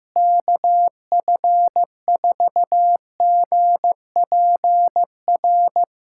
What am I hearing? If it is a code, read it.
Morse code, heard as a keyed tone, about 15 wpm, 700 Hz. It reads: KF4GPR
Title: USA Callsign DE Callsign - 6 Characters - 15wpm